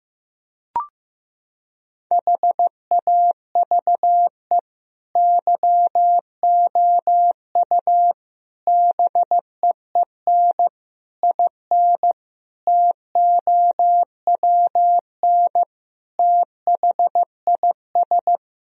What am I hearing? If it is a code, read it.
Morse code, heard as a keyed tone, about 15 wpm, 700 Hz. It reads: HAVE YOU BEEN IN TOWN THIS
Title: Sentences from Top 1000 Words - Review Part 2 - 15wpm